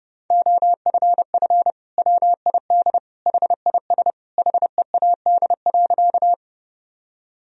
Morse code, heard as a keyed tone, about 32 wpm, 700 Hz. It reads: OFF WID HIS HEAD.
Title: Sentences from Top 300 Words and Abbreviations 30wpm